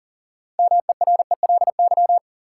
Morse code, heard as a keyed tone, about 40 wpm, 700 Hz. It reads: MERELY
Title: Sentences from Top 1400 Words - New Word Drill - 40wpm